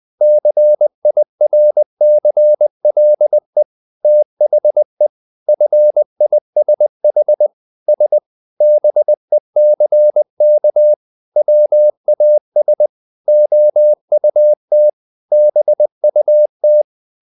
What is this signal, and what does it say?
Morse code, heard as a keyed tone, about 20 wpm, 600 Hz. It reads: CIRCLE THE FISH S BECK WAS OUT BUT
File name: the old man and the sea part 2 20wpm